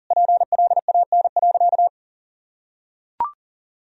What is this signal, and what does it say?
Morse code, heard as a keyed tone, about 40 wpm, 700 Hz. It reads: PLAN.
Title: Sentences from Top 1000 Words - Review Part 2 - 40wpm